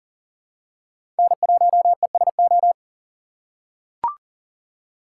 Morse code, heard as a keyed tone, about 40 wpm, 700 Hz. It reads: N1ESO
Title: USA Call Signs - 5 Characters - 3x Character Spacing - 40wpm